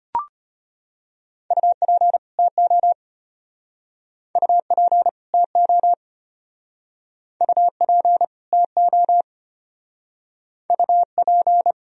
Morse code, heard as a keyed tone, about 31 wpm, 700 Hz. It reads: UP TO UP TO UP TO UP
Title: Speed-Racing - Sets of 2 Words 25wpm